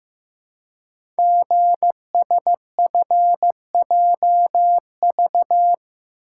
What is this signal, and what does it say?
Morse code, heard as a keyed tone, about 15 wpm, 690 Hz. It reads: GSFJV
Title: Code Groups - 5 Letters - 15wpm